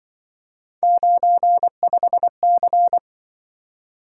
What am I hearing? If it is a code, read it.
Morse code, heard as a keyed tone, about 25 wpm, 700 Hz. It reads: 95C
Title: Speed-Racing - Three Letter-Number Combinations - 20wpm